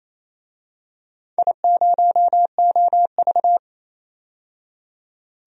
Morse code, heard as a keyed tone, about 28 wpm, 700 Hz. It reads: I0OV
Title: DXCC-100 Call Signs - 4 Characters - 28wpm